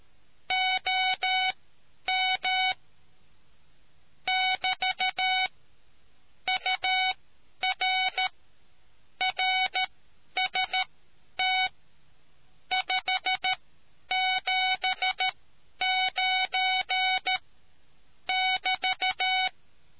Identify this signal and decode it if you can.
Morse code, heard as a keyed tone, about 13 wpm, spaced as for 10 wpm, 750 Hz. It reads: OM <BT> UR RST 579 <BT>